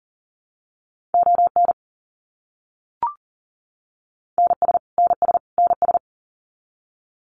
Morse code, heard as a keyed tone, about 40 wpm, 700 Hz. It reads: ON NS NS NS
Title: Canadian Provinces and Territories - Repeated 3x - 40wpm